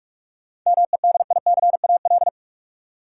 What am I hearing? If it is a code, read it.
Morse code, heard as a keyed tone, about 42 wpm, 700 Hz. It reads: MEDICAL
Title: Sentences from Top 800 Words - New Word Drill - 45wpm